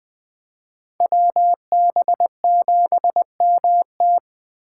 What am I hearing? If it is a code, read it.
Morse code, heard as a keyed tone, about 24 wpm, 700 Hz. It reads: WB7MT
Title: Speed Racing - USA 2x2 Call Signs 20wpm